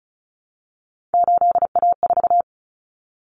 Morse code, heard as a keyed tone, about 34 wpm, 700 Hz. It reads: 8A4
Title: Common Cut Numbers - 3-Digit 35wpm